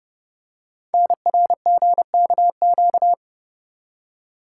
Morse code, heard as a keyed tone, about 30 wpm, 700 Hz. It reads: NRGKQ
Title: Code Groups - 5 Letters - 30wpm